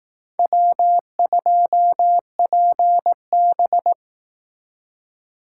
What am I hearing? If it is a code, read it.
Morse code, heard as a keyed tone, about 18 wpm, 700 Hz. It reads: W2PB
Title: Speed Racing - USA 1x2 Call Signs 15wpm